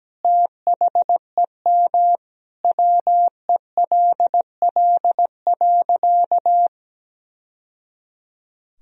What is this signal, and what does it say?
Morse code, heard as a keyed tone, about 17 wpm, 700 Hz. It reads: THEM WELL.
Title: Sentences from Top 1000 Words - Review Part 1 - 17wpm